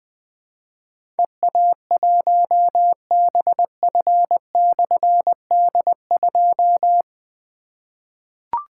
Morse code, heard as a keyed tone, about 20 wpm, 700 Hz. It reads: EA1BF/D2
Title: Difficult Call Signs 20wpm